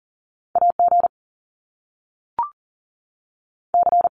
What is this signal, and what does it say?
Morse code, heard as a keyed tone, about 40 wpm, 700 Hz. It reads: AG C
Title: Top Two Letter Combinations in Words 40wpm